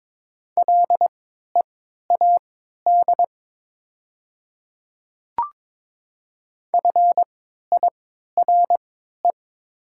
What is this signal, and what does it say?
Morse code, heard as a keyed tone, about 22 wpm, spaced as for 12 wpm, 700 Hz. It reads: LEAD FIRE
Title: 3x Character Spacing - Top 500 Words - 22wpm